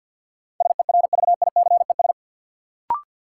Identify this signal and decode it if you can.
Morse code, heard as a keyed tone, about 53 wpm, 700 Hz. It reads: SERVICES
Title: Sets of 5 Words - New Word Drill - 50wpm